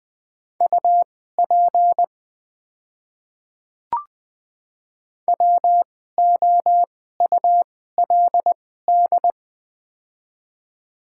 Morse code, heard as a keyed tone, about 20 wpm, spaced as for 14 wpm, 700 Hz. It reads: UP WOULD
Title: 2x Character Spacing - Top 300 Words - 20wpm